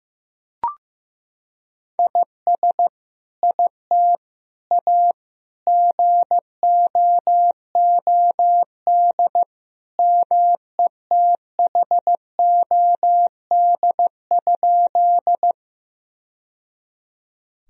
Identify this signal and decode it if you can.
Morse code, heard as a keyed tone, about 15 wpm, 700 Hz. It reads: IS IT A GOOD METHOD?
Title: Sentences from Top 1000 Words - Review Part 3 - 15wpm